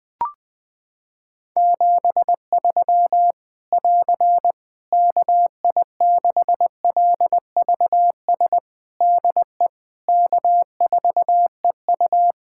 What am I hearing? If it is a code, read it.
Morse code, heard as a keyed tone, about 20 wpm, 700 Hz. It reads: 73 <AR> KI6LVS DE K4EU
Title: K1USN SST QSO Practice 20wpm